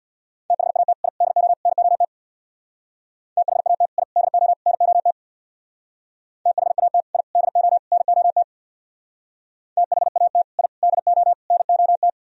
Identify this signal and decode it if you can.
Morse code, heard as a keyed tone, about 63 wpm, 700 Hz. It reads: THAT I DO NOT THAT I DO NOT THAT I DO NOT THAT I DO NOT